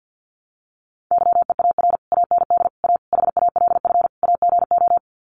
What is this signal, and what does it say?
Morse code, heard as a keyed tone, about 50 wpm, 700 Hz. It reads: YEAR AND A HALF AGO
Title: Sets of 5 Words Encore 50wpm